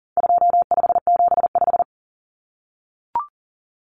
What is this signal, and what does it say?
Morse code, heard as a keyed tone, about 40 wpm, 700 Hz. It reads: 2575
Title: Numbers - 4-Digit 40wpm